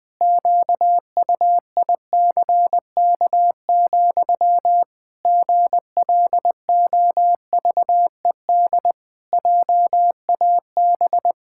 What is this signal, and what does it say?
Morse code, heard as a keyed tone, about 20 wpm, 700 Hz. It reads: QUICK, GLOVED JAB
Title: Pangram Challenge v2 - 20wpm